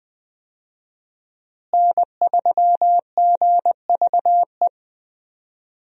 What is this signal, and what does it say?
Morse code, heard as a keyed tone, about 21 wpm, 700 Hz. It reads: N3GVE
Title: USA Call Signs - 5 Characters - Repeated 3x - 20wpm